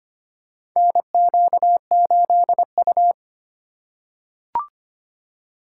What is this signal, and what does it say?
Morse code, heard as a keyed tone, about 25 wpm, 700 Hz. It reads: NQ8U
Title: USA Call Signs - 4 Characters - 2x Character Spacing - 25wpm